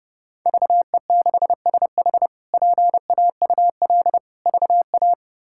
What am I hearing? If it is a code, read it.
Morse code, heard as a keyed tone, about 30 wpm, 700 Hz. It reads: VE6SH PAUL VA
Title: K1USN SST QSO Practice - 30wpm